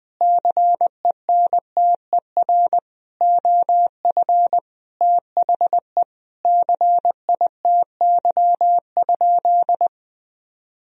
Morse code, heard as a keyed tone, about 21 wpm, 700 Hz. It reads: CENTER OF THE CITY?
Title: Sentences from Top 1000 Words - Review Part 2 - 20wpm